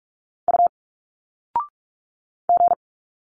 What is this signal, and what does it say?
Morse code, heard as a keyed tone, about 46 wpm, 700 Hz. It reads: U G
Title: Single Letters 45wpm